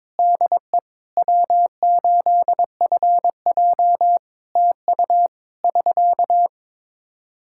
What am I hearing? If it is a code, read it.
Morse code, heard as a keyed tone, about 22 wpm, 700 Hz. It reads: DE W8FJ TU <SK>
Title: Practice QSO - Exchange 4 - Ending - 22wpm